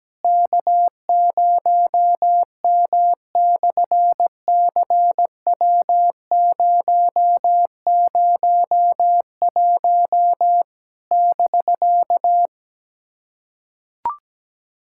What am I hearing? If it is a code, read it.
Morse code, heard as a keyed tone, about 17 wpm, 700 Hz. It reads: K0M/CW001 <BK>
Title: SOTA QSO Practice 17wpm